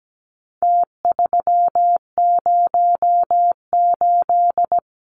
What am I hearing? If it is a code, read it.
Morse code, heard as a keyed tone, about 17 wpm, 700 Hz. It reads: T308
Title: Numbers - Contest Style - 4-Digit 17wpm